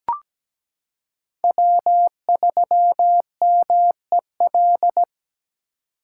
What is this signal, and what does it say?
Morse code, heard as a keyed tone, about 17 wpm, 700 Hz. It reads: W3MEL
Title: K1USN SST QSO Practice - 17wpm